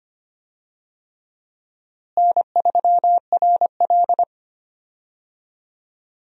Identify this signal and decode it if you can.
Morse code, heard as a keyed tone, about 25 wpm, 700 Hz. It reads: N3RL
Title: USA Call Signs - 4 Characters - Repeated 3x - 25wpm